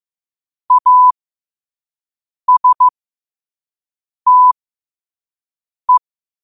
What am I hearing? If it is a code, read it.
Morse code, heard as a keyed tone, about 15 wpm, spaced as for 5 wpm, 1000 Hz. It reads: ASTE
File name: 5wpm